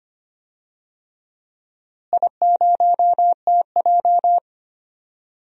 Morse code, heard as a keyed tone, about 25 wpm, 700 Hz. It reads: I0TJ
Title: DXCC-100 Call Signs - 4 Characters - Repeated 3x - 25wpm